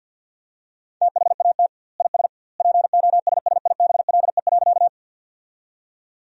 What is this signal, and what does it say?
Morse code, heard as a keyed tone, about 50 wpm, 700 Hz. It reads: THAT IS POSSIBLE.
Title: 2x Word-Spacing - Sentences from Top 500 Words - 50wpm